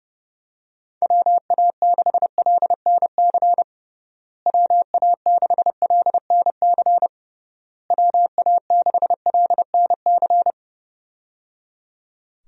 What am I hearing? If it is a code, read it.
Morse code, heard as a keyed tone, about 30 wpm, 700 Hz. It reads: WA6LNC WA6LNC WA6LNC
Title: USA Call Signs - 6 Characters - Repeated 3x - 30wpm